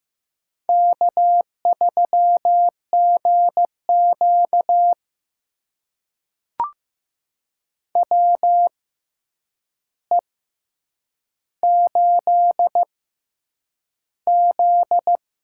Morse code, heard as a keyed tone, about 15 wpm, spaced as for 8 wpm, 700 Hz. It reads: K3GQ WE8Z
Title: USA Call Signs - 6x Character Spacing - 15wpm